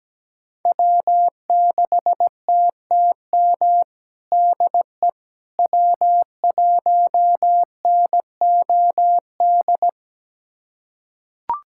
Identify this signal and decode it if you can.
Morse code, heard as a keyed tone, about 17 wpm, 700 Hz. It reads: W6TTM DE W1NOD
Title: USA Callsign DE Callsign - 5 Characters - 17wpm